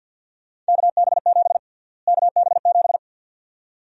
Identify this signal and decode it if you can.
Morse code, heard as a keyed tone, about 47 wpm, 700 Hz. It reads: KB7 KB7
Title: USA Call Sign Prefixes - Repeated 2x - 50wpm